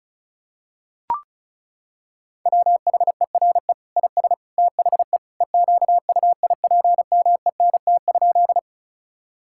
Morse code, heard as a keyed tone, about 34 wpm, 700 Hz. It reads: WHERE IS THE EQUIPMENT?
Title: Sentences from Top 1000 Words - Review Part 5 - 35wpm